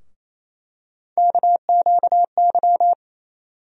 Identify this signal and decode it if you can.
Morse code, heard as a keyed tone, about 28 wpm, 700 Hz. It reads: KQY
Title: USA Call Sign Postfixes - 3 Letters 28wpm